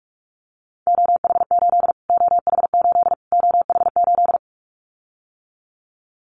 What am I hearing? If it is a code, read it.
Morse code, heard as a keyed tone, about 44 wpm, 700 Hz. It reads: OH8 OH8 OH8
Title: Callsign Prefixes for Top 25 DXCC - Repeated 3x - 45wpm